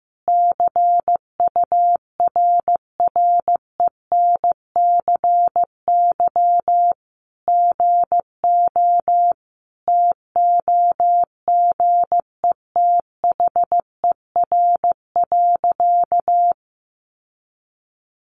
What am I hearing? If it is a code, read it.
Morse code, heard as a keyed tone, about 15 wpm, 700 Hz. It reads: CURRENCY GO TOGETHER.
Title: Sentences from Top 1500 Words 15wpm